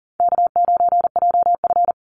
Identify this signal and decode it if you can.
Morse code, heard as a keyed tone, about 38 wpm, 700 Hz. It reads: K9JF
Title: Easy Call Signs - 40wpm